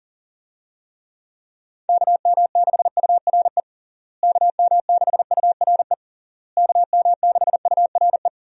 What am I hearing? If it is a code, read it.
Morse code, heard as a keyed tone, about 40 wpm, 700 Hz. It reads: KM6URE KM6URE KM6URE
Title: USA Call Signs - 6 Characters - Repeated 3x - 40wpm